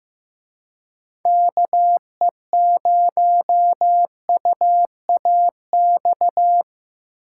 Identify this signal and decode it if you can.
Morse code, heard as a keyed tone, about 18 wpm, 700 Hz. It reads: KE0UAX
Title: Speed Racing - USA 2x3 Call Signs 15wpm